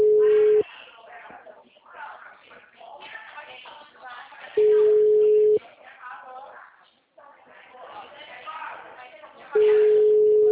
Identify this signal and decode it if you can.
Ringback tone (Europe, 425 Hz), with speech.